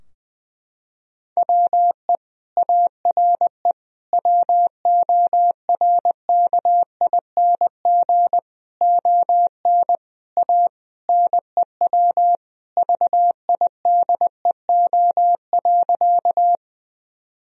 Morse code, heard as a keyed tone, about 20 wpm, 700 Hz. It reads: WE ARE WORKING ON A NEW VIDEO.
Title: Sentences from Top 1000 Words - Review Part 4 - 20wpm